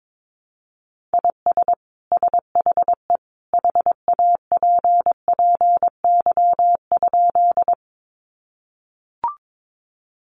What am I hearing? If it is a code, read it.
Morse code, heard as a keyed tone, about 22 wpm, 700 Hz. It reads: IS SHE HAPPY?
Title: Sentences from Top 1000 Words 22wpm